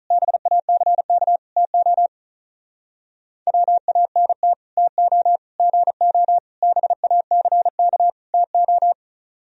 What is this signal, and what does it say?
Morse code, heard as a keyed tone, about 45 wpm, 700 Hz. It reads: BACK TO WANT TO GO BACK TO